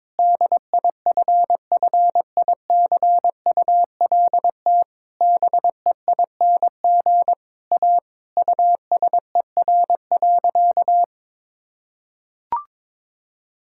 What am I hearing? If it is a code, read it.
Morse code, heard as a keyed tone, about 22 wpm, 700 Hz. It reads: DIFFICULT BEING A USER.
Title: Sentences from Top 1000 Words - Review Part 5 - 22wpm